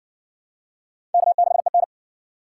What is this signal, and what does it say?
Morse code, heard as a keyed tone, about 60 wpm, 700 Hz. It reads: K6EN